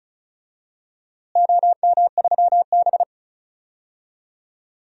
Morse code, heard as a keyed tone, about 36 wpm, 700 Hz. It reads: OM3B
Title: DXCC-100 Call Signs - 4 Characters - Repeated 3x - 35wpm